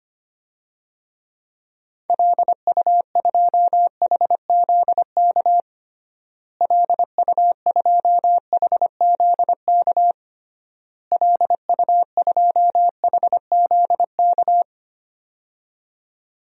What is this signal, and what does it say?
Morse code, heard as a keyed tone, about 25 wpm, 700 Hz. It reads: LU2HZK LU2HZK LU2HZK
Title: DXCC-100 Call Signs - 6 Characters - Repeated 3x - 25wpm